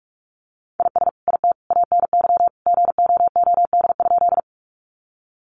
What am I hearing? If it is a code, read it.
Morse code, heard as a keyed tone, about 44 wpm, 700 Hz. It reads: IS IT ANY GOOD?
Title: Sentences from Top 200 Words 45wpm